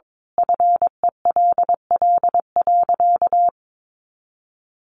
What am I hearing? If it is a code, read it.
Morse code, heard as a keyed tone, about 22 wpm, 700 Hz. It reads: FELL.
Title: A Sound Of Thunder 22wpm